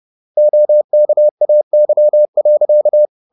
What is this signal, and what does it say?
Morse code, heard as a keyed tone, about 30 wpm, 600 Hz. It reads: OKAY.
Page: Ferie 30 wpm